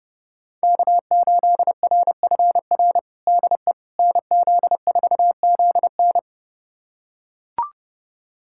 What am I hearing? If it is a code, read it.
Morse code, heard as a keyed tone, about 30 wpm, 700 Hz. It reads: K8RFR DE NZ4ZN
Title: USA Callsign DE Callsign - 5 Characters - 30wpm